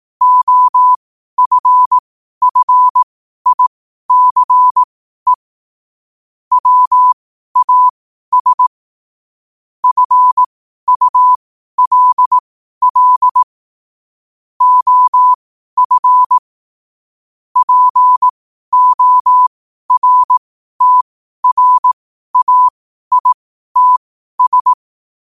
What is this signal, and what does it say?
Morse code, heard as a keyed tone, about 18 wpm, spaced as for 12 wpm, 1000 Hz. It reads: OFFICE WAS FULL OF PORTRAITS
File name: qotdfiles / 12wpm